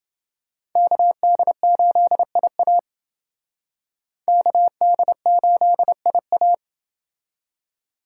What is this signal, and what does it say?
Morse code, heard as a keyed tone, about 28 wpm, 700 Hz. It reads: KD8IA KD8IA